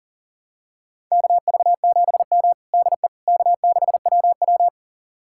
Morse code, heard as a keyed tone, about 40 wpm, 700 Hz. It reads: KV7M DE K6WW
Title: USA Callsign DE Callsign - 4 Characters - 40wpm